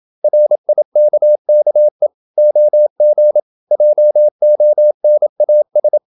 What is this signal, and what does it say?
Morse code, heard as a keyed tone, about 27 wpm, 600 Hz. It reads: RIKKE OG JONAS
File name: Ferie QRQ 27wpm